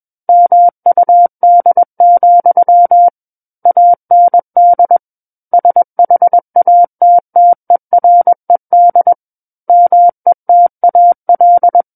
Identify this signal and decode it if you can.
Morse code, heard as a keyed tone, about 21 wpm, 700 Hz. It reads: MUD, AND SHATTERED METAL
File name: Book One / 12-Chapter 12 - 21 WPM